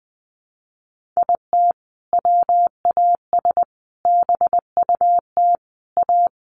Morse code, heard as a keyed tone, about 20 wpm, 700 Hz. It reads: IT WAS BUT A
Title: Sentences from Top 1500 Words 20wpm